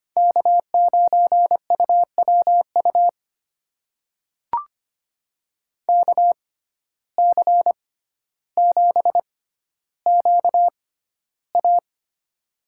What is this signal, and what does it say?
Morse code, heard as a keyed tone, about 25 wpm, spaced as for 13 wpm, 700 Hz. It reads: K9UWU KC7QA
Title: USA Call Signs - 5 Characters - 6x Character Spacing - 25wpm